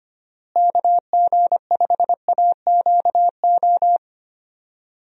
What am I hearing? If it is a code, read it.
Morse code, heard as a keyed tone, about 25 wpm, 700 Hz. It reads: KG5AQO
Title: USA Call Signs - 6 Characters - Repeated 3x - 25wpm